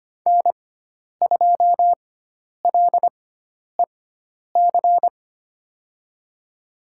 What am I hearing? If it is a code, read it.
Morse code, heard as a keyed tone, about 25 wpm, spaced as for 10 wpm, 700 Hz. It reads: N2LEC
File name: Lesson 16g - Letter c - Call Signs 25wpm